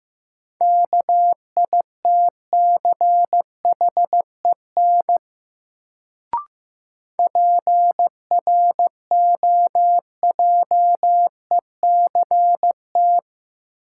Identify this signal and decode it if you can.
Morse code, heard as a keyed tone, about 15 wpm, 700 Hz. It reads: KITCHEN PROJECT
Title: Sentences from Top 1400 Words - New Word Drill Alt - 15wpm